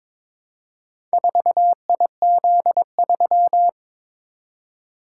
Morse code, heard as a keyed tone, about 22 wpm, 700 Hz. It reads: 4IZ3
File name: practice / 22wpm / Four Letter-Number Combinations - 22wpm